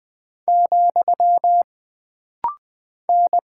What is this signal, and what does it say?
Morse code, heard as a keyed tone, about 20 wpm, 700 Hz. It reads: , N
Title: Single Character (Letter:Number:Punctuation:Prosign) - Rapid-Fire - 20wpm